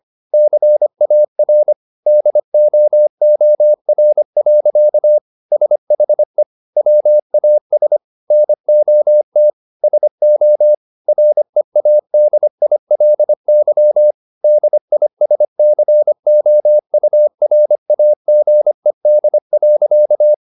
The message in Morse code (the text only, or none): CAR DOOR. SHE WAS NOT SO READILY DISCOURAGED.